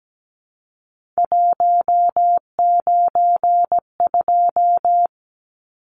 Morse code, heard as a keyed tone, about 17 wpm, 700 Hz. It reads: 192
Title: Numbers - 3-Digit 17wpm